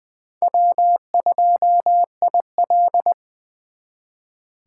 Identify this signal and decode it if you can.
Morse code, heard as a keyed tone, about 23 wpm, 700 Hz. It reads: W2IL
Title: Speed Racing - USA 1x2 Call Signs 20wpm